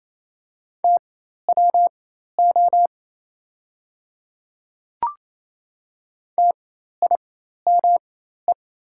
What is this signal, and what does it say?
Morse code, heard as a keyed tone, about 27 wpm, spaced as for 13 wpm, 690 Hz. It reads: TWO TIME
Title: 4x Character Spacing - Top 100 Words - 28wpm